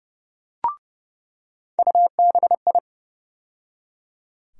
Morse code, heard as a keyed tone, about 30 wpm, 700 Hz. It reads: UBI
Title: USA Call Sign Postfixes - 3 Letters 30wpm